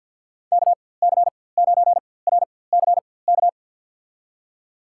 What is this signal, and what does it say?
Morse code, heard as a keyed tone, about 52 wpm, spaced as for 23 wpm, 700 Hz. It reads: KC9RCK
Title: USA Call Signs - 6 Characters - 4x Character Spacing - 50wpm